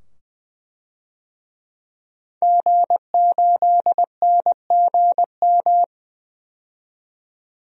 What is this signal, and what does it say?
Morse code, heard as a keyed tone, about 21 wpm, 700 Hz. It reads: G8NGM
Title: DXCC-100 Call Signs - 5 Characters - 20wpm